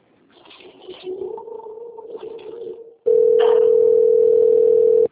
Ringback tone (North America), 440 + 480 Hz.